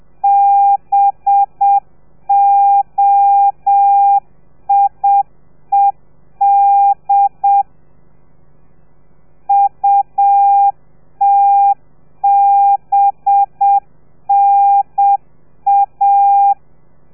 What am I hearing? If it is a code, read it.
Morse code, heard as a keyed tone, about 7 wpm, 800 Hz. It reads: BOIED UTBNA
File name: telegrafia / i2tfj / Lez4 30